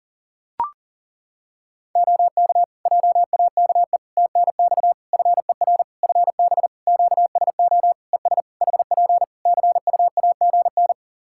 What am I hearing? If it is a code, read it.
Morse code, heard as a keyed tone, about 37 wpm, 700 Hz. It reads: OK JAKE TNX FER FB QSO ES HP CUAGN
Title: Practice QSO - Exchange 4 - Ending - 40wpm